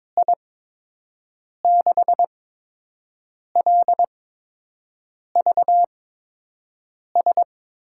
Morse code, heard as a keyed tone, about 22 wpm, spaced as for 6 wpm, 700 Hz. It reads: I6LVS
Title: USA Call Signs - 6 Characters - 8x Character Spacing - 22wpm